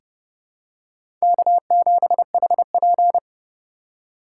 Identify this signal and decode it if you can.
Morse code, heard as a keyed tone, about 30 wpm, 700 Hz. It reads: K7HP
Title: USA Call Signs - 4 Characters - 2x Character Spacing - 30wpm